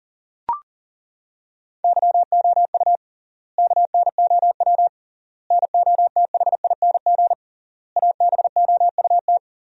Morse code, heard as a keyed tone, about 40 wpm, 700 Hz. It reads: YOU KNOW NOTHING ABOUT
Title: 3x Word-Spacing - Sentences from Top 500 Words - 40wpm